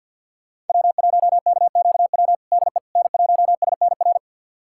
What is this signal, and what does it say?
Morse code, heard as a keyed tone, about 48 wpm, 700 Hz. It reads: W1KQW DE N1SNR
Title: USA Callsign DE Callsign - 5 Characters - 50wpm